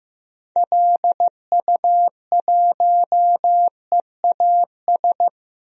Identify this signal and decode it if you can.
Morse code, heard as a keyed tone, about 15 wpm, 700 Hz. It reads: LU1EAS
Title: DXCC-100 Call Signs - 6 Characters - 15wpm